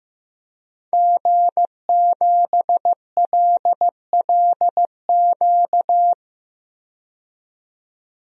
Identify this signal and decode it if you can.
Morse code, heard as a keyed tone, about 15 wpm, 700 Hz. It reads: G7LLQ
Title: DXCC-100 Call Signs - 5 Characters - 15wpm